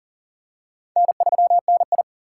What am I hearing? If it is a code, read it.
Morse code, heard as a keyed tone, about 43 wpm, 700 Hz. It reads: N3NI
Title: USA Call Signs - 4 Characters - 2x Character Spacing - 40wpm